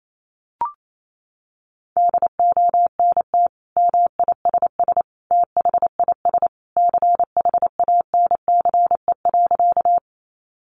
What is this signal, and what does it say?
Morse code, heard as a keyed tone, about 28 wpm, 700 Hz. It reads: DONT MISS THIS CHANCE.
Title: Sentences from Top 800 Words 28wpm